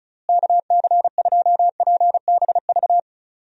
Morse code, heard as a keyed tone, about 36 wpm, 700 Hz. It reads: KC2PBV
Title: USA Call Signs - 6 Characters - Repeated 3x - 35wpm